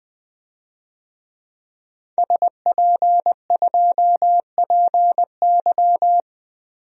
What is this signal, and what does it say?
Morse code, heard as a keyed tone, about 20 wpm, 700 Hz. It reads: SP2PY
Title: DXCC-100 Call Signs - 5 Characters - Repeated 2x - 20wpm